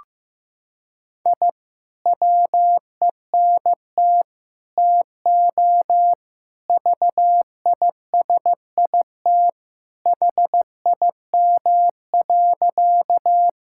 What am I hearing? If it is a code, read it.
Morse code, heard as a keyed tone, about 14 wpm, 700 Hz. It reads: I WENT TO VISIT HIM.
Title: Sentences from Top 1000 Words - Review Part 6 - 15wpm